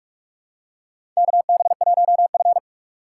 Morse code, heard as a keyed tone, about 44 wpm, 700 Hz. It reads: KB1F
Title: USA Call Signs - 4 Characters - 6x Character Spacing - 45wpm